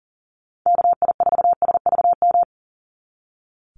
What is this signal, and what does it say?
Morse code, heard as a keyed tone, about 40 wpm, 700 Hz. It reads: KI4SVM
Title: NA SOTA Activator Callsigns 40wpm